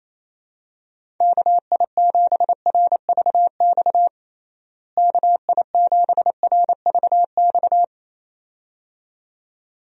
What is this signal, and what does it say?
Morse code, heard as a keyed tone, about 28 wpm, 700 Hz. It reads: KI7RVX KI7RVX
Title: USA Call Signs - 6 Characters - Repeated 2x - 28wpm